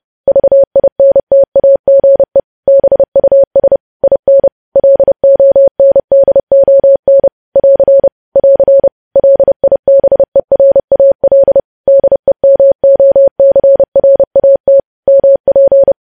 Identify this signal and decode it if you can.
Morse code, heard as a keyed tone, about 30 wpm, 550 Hz. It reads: VINTAGE BUS IN LONDON <AR> <AR> LIBERAL DEMOCRAT MP
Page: news headlines in morse code at 30 wpm